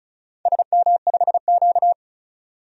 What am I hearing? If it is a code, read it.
Morse code, heard as a keyed tone, about 36 wpm, 700 Hz. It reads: SM5Q
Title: DXCC-100 Call Signs - 4 Characters - 35wpm